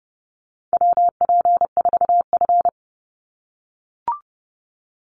Morse code, heard as a keyed tone, about 30 wpm, 700 Hz. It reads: WP4F
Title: Call Signs - 30wpm